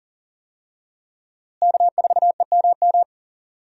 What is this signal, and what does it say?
Morse code, heard as a keyed tone, about 40 wpm, 700 Hz. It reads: K4EMM
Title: USA Call Signs - 5 Characters - Repeated 2x - 40wpm